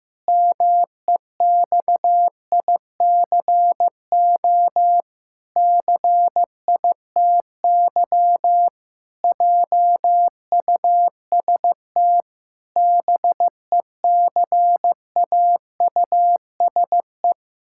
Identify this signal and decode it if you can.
Morse code, heard as a keyed tone, about 15 wpm, 700 Hz. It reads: MEXICO CITY JUST BECAUSE
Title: Pangram Challenge v2 - 15wpm